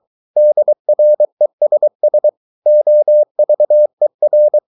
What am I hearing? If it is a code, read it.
Morse code, heard as a keyed tone, about 23 wpm, 600 Hz. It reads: DRESS OVER
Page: Romanen Never af Ken Follet på engelsk.